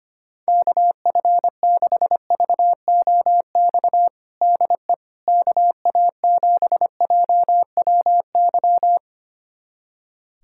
Morse code, heard as a keyed tone, about 25 wpm, 700 Hz. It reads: KF6VOX DE KA7JWY
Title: USA Callsign DE Callsign - 6 Characters - 25wpm